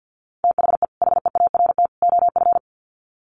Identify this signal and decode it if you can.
Morse code, heard as a keyed tone, about 50 wpm, 700 Hz. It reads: THE HEART OF
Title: Sets of 3 Words 50wpm